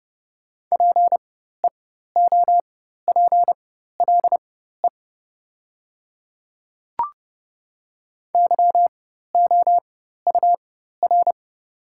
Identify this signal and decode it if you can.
Morse code, heard as a keyed tone, about 30 wpm, spaced as for 14 wpm, 700 Hz. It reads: PEOPLE YOUR
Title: 4x Character Spacing - Top 100 Words - 30wpm